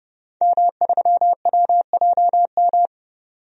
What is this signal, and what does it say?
Morse code, heard as a keyed tone, about 30 wpm, 700 Hz. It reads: M3WJM
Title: DXCC-100 Call Signs - 5 Characters - 30wpm